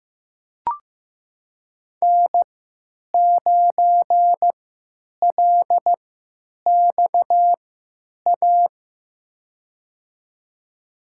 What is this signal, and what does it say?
Morse code, heard as a keyed tone, about 15 wpm, spaced as for 9 wpm, 700 Hz. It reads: N9LXA
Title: USA Call Signs - 5 Characters - 3x Character Spacing - 15wpm